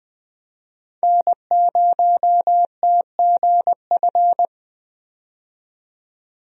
Morse code, heard as a keyed tone, about 21 wpm, 700 Hz. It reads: N0TGF
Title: USA Call Signs - 5 Characters - 20wpm